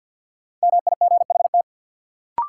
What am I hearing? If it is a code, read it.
Morse code, heard as a keyed tone, about 53 wpm, 700 Hz. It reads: MIGHT
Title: Sentences from Top 300 Words - New Word Drill - 50wpm